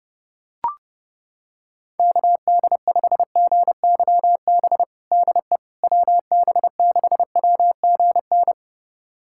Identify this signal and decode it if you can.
Morse code, heard as a keyed tone, about 30 wpm, 700 Hz. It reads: KD5GYB DE WB6WGN
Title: USA Callsign DE Callsign - 6 Characters - 30wpm